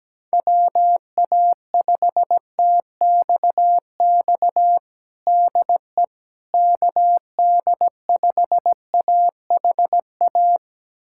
Morse code, heard as a keyed tone, about 17 wpm, 700 Hz. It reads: WA5TXX DE KD5AHA
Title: USA Callsign DE Callsign - 6 Characters - 17wpm